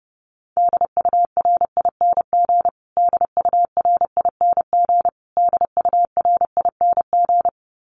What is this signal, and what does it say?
Morse code, heard as a keyed tone, about 29 wpm, 700 Hz. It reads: DURING DURING DURING
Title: Top 500 Words Repeated 30wpm